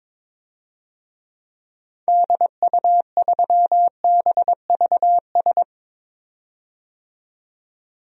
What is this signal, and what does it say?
Morse code, heard as a keyed tone, about 22 wpm, 700 Hz. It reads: DU3BVS
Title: DXCC-100 Call Signs - 6 Characters - 22wpm